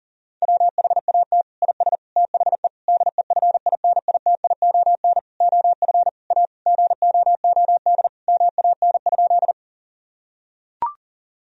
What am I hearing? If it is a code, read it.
Morse code, heard as a keyed tone, about 40 wpm, 700 Hz. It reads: WHAT IS THE DEFINITION OF A GOOD MAN?
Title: Sentences from Top 1000 Words - Review Part 4 - 40wpm